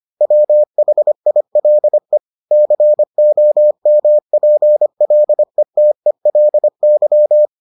Morse code, heard as a keyed tone, about 25 wpm, 600 Hz. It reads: WHILE COMPLETELY